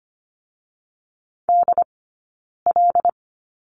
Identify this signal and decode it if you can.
Morse code, heard as a keyed tone, about 25 wpm, 700 Hz. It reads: D L
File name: Lesson 17d - Letter u - Two Characters 25wpm